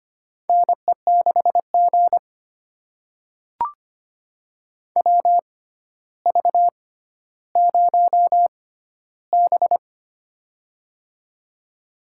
Morse code, heard as a keyed tone, about 25 wpm, spaced as for 13 wpm, 700 Hz. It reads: NE6G WV0B
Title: USA Call Signs - 4 Characters - 6x Character Spacing - 25wpm